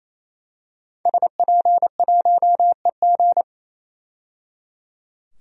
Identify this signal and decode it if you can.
Morse code, heard as a keyed tone, about 28 wpm, 700 Hz. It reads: SP1EG
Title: DXCC-100 Call Signs - 5 Characters - 28wpm